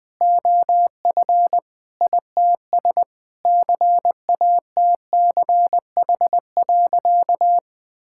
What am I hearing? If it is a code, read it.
Morse code, heard as a keyed tone, about 21 wpm, 700 Hz. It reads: OF ITS CATCH.
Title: Sentences with 2-Word Phrases - 20wpm